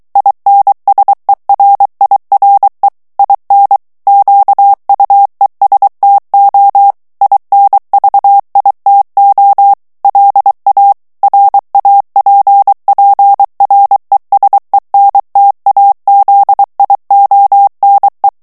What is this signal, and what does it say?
Morse code, heard as a keyed tone, about 23 wpm, 800 Hz. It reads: INSERIRE IN QUESTO INVITO LA RAPPRESENTAZIONE
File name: morse mp3 / ITALIEN texte clair 1 vitesse 22 mots minute